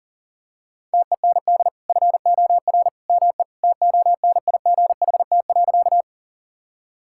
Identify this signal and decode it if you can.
Morse code, heard as a keyed tone, about 40 wpm, 700 Hz. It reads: TEND FOR ME TONIGHT.
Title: Sentences from Top 1000 Words - Review Part 5 - 40wpm